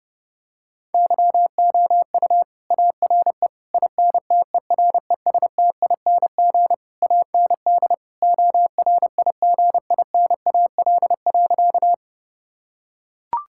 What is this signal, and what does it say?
Morse code, heard as a keyed tone, about 30 wpm, 700 Hz. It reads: YOU ARE INTERESTING AND ORIGINAL.
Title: Sentences from Top 1000 Words - Review Part 6 - 30wpm